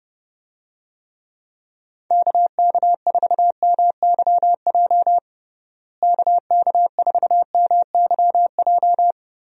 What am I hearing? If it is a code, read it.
Morse code, heard as a keyed tone, about 30 wpm, 700 Hz. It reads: KK4MYJ KK4MYJ
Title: USA Call Signs - 6 Characters - Repeated 2x - 30wpm